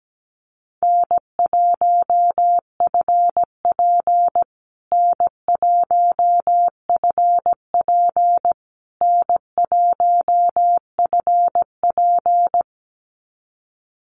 Morse code, heard as a keyed tone, about 17 wpm, 700 Hz. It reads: N1FP N1FP N1FP
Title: USA Call Signs Repeated 3x 17wpm